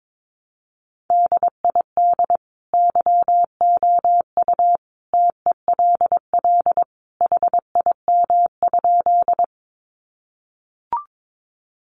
Morse code, heard as a keyed tone, about 22 wpm, 700 Hz. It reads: DID YOU TELL HIM?
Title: Sentences from Top 200 Words 22wpm